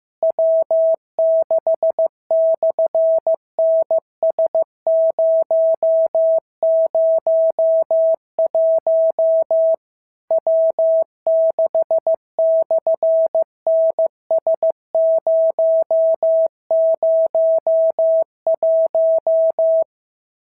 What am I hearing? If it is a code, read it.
Morse code, heard as a keyed tone, about 15 wpm, 650 Hz. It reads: W6/NS001 W6/NS001
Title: SOTA QSO Practice 15wpm